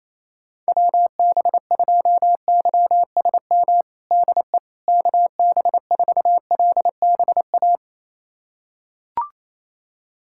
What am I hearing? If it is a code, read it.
Morse code, heard as a keyed tone, about 28 wpm, 700 Hz. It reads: WB2YSM DE KB4LBA
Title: USA Callsign DE Callsign 28wpm